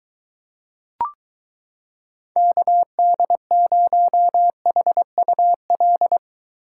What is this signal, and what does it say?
Morse code, heard as a keyed tone, about 23 wpm, 700 Hz. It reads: KD0HUL